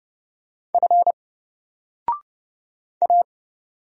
Morse code, heard as a keyed tone, about 31 wpm, 700 Hz. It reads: F A
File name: Single-Letter-Number-30wpm